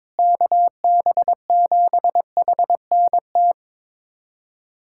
Morse code, heard as a keyed tone, about 22 wpm, 700 Hz. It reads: KB7HNT
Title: USA Callsign DE Callsign - 6 Characters - 22wpm